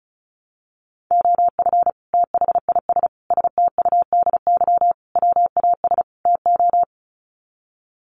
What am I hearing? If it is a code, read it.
Morse code, heard as a keyed tone, about 34 wpm, 700 Hz. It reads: OF THIS STUDY WAS TO
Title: Sets of 5 Words 35wpm